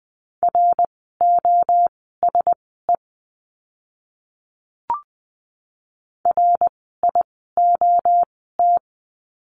Morse code, heard as a keyed tone, about 20 wpm, spaced as for 14 wpm, 700 Hz. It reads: ROSE RIOT
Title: 061 Lesson 11f - Letter h - Words 20wpm